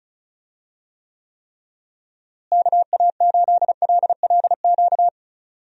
Morse code, heard as a keyed tone, about 36 wpm, 700 Hz. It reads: KA8LLQ
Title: USA Call Signs - 6 Characters - Repeated 3x - 35wpm